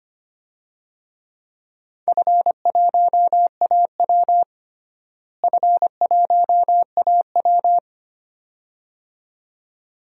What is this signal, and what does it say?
Morse code, heard as a keyed tone, about 25 wpm, 700 Hz. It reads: F1AW F1AW
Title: DXCC-100 Call Signs - 4 Characters - Repeated 2x - 25wpm